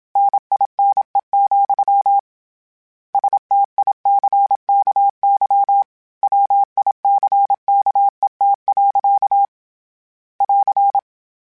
Morse code, heard as a keyed tone, about 27 wpm, 800 Hz. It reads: NINE, STICKY WICKET. <AR>
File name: Misc / Misc-mp3s / Word Pairs @25